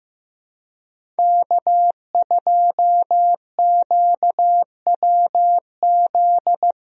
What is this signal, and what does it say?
Morse code, heard as a keyed tone, about 15 wpm, 700 Hz. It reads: K2QWZ
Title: USA Call Signs - 5 Characters - Repeated 2x - 15wpm